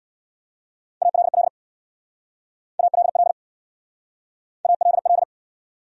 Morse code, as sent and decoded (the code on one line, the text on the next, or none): .- .-.. .-.. / .- .-.. .-.. / .- .-.. .-..
ALL ALL ALL